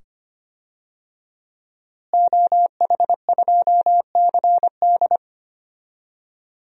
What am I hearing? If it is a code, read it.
Morse code, heard as a keyed tone, about 25 wpm, 700 Hz. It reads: OH2CD
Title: DXCC-100 Call Signs - 5 Characters - Repeated 2x - 25wpm